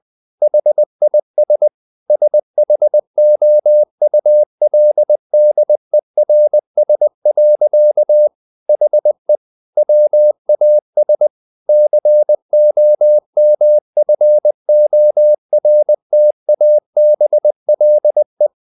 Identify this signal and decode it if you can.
Morse code, heard as a keyed tone, about 20 wpm, 600 Hz. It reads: HIS SHOULDERS. HE WAS COMFORTABLE
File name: the old man and the sea / the old man and the sea part 1 20wpm